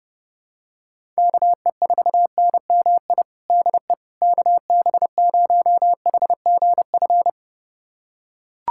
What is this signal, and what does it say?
Morse code, heard as a keyed tone, about 30 wpm, 700 Hz. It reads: KE4NMI DE KB0HGF
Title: USA Callsign DE Callsign - 6 Characters - 30wpm